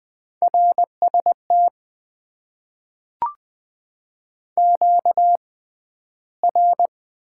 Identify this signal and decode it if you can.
Morse code, heard as a keyed tone, about 20 wpm, spaced as for 12 wpm, 700 Hz. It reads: RST QR
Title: Top 90 QSO Elements - 6x Character Spacing - 20wpm